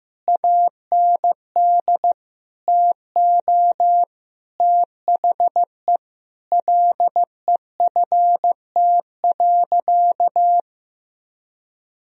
Morse code, heard as a keyed tone, about 15 wpm, 700 Hz. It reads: AND TO THE LEFT.
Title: Sentences from Top 1000 Words - Review Part 1 - 15wpm